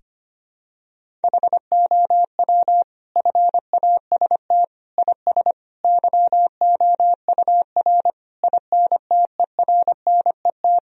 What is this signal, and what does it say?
Morse code, heard as a keyed tone, about 25 wpm, 700 Hz. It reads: HOW FAST IS YOUR INTERNET
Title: Sentences from Top 1000 Words - Review Part 4 - 25wpm